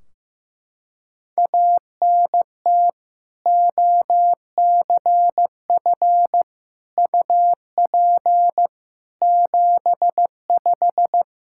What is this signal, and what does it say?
Morse code, heard as a keyed tone, about 15 wpm, 700 Hz. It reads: ANT OCF UP 75
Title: Practice QSO - Exchange 2 - Info - 15wpm